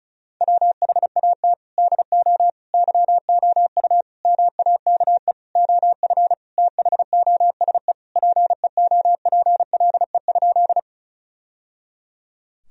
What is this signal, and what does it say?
Morse code, heard as a keyed tone, about 34 wpm, 700 Hz. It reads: WHAT DO YOU MAKE OF THOSE PEOPLE?
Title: Sentences from Top 1000 Words - Review Part 1 - 35wpm